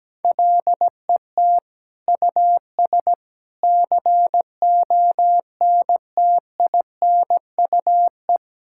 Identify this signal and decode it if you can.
Morse code, heard as a keyed tone, about 17 wpm, 700 Hz. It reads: LET US CONTINUE
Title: Sentences from Top 1000 Words - Review Part 6 - 17wpm